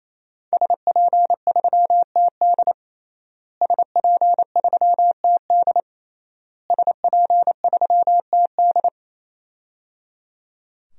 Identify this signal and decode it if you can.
Morse code, heard as a keyed tone, about 27 wpm, 700 Hz. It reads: SP3TD SP3TD SP3TD
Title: DXCC-100 Call Signs - 5 Characters - Repeated 3x - 28wpm